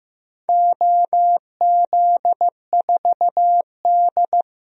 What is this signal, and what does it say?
Morse code, heard as a keyed tone, about 15 wpm, 700 Hz. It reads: OZ4D
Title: DXCC-100 Call Signs - 4 Characters - 15wpm